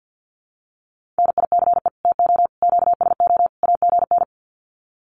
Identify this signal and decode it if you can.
Morse code, heard as a keyed tone, about 50 wpm, 700 Hz. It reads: NICE TO QSO AGN
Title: Common QSO Phrases 50wpm